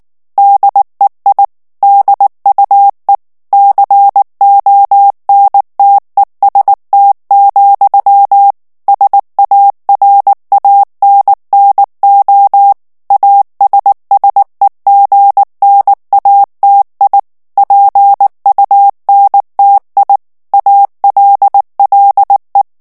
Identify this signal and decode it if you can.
Morse code, heard as a keyed tone, about 19 wpm, 800 Hz. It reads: DEI DUE CONTEST, SARANNO ASSEGNATI PUNTI ALLE
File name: morse mp3 / ITALIEN texte clair 2 vitesse 18 mots minute